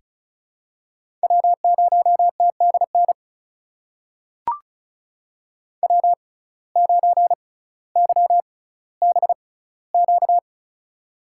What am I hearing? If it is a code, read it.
Morse code, heard as a keyed tone, about 36 wpm, spaced as for 18 wpm, 700 Hz. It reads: W0TDN W9YBQ
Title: USA Call Signs - 6x Character Spacing - 35wpm